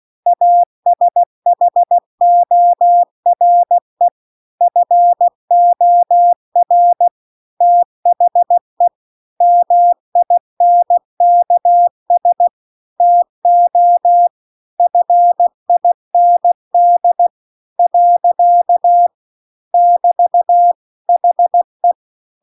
Morse code, heard as a keyed tone, about 16 wpm, 700 Hz. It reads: ASHORE FOR THE MINKS TO FIND . <BT> HE
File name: CW-The-Big-Twoheaded-River-2-16WPM